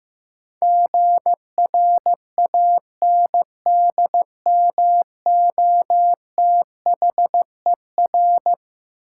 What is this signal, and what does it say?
Morse code, heard as a keyed tone, about 16 wpm, 700 Hz. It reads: GRANDMOTHER
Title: Sentences from Top 1200 Words - New Word Drill Alt - 15wpm